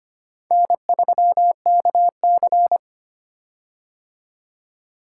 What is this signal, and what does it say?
Morse code, heard as a keyed tone, about 26 wpm, 700 Hz. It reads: N3KC
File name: practice / 25wpm / Four Letter-Number Combinations - 25wpm